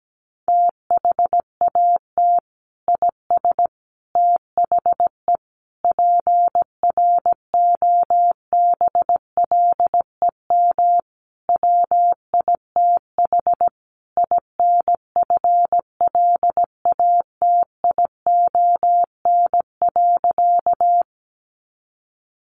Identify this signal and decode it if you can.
Morse code, heard as a keyed tone, about 17 wpm, 700 Hz. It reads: THAT IS THE PROBLEM WITH INFLATION.
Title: Sentences from Top 1100 Words 17wpm